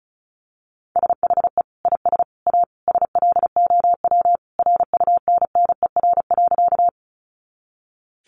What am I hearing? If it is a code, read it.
Morse code, heard as a keyed tone, about 34 wpm, 700 Hz. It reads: SHE IS A SLOW RUNNER.
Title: Additional Easy Phrases 35wpm